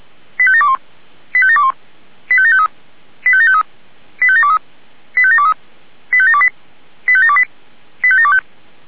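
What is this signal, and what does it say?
Five-tone selective call (ZVEI), heard as quick runs of five tones, tones of about 80 ms. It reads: 85721 85721 85753 85753 85723 85723 85728 85728 85725
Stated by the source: Five-tone ZVEI selective call 85724 received on an SDR.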